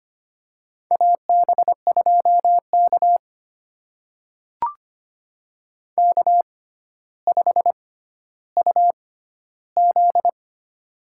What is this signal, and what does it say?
Morse code, heard as a keyed tone, about 25 wpm, spaced as for 13 wpm, 700 Hz. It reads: AB2K K5UZ
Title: USA Call Signs - 4 Characters - 6x Character Spacing - 25wpm